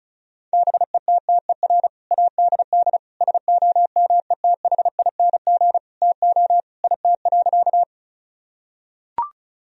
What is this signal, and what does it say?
Morse code, heard as a keyed tone, about 36 wpm, 700 Hz. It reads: BETTER ADD SOMETHING TO IT.
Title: Sentences from Top 1000 Words - Review Part 1 - 35wpm